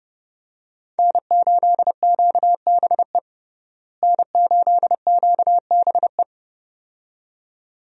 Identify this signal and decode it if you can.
Morse code, heard as a keyed tone, about 32 wpm, 700 Hz. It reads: N8QBE N8QBE
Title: USA Call Signs - 5 Characters - Repeated 2x - 30wpm